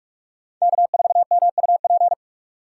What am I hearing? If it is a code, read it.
Morse code, heard as a keyed tone, about 42 wpm, 700 Hz. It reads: K4MUP
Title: USA Call Signs - 2x Character Spacing - 45wpm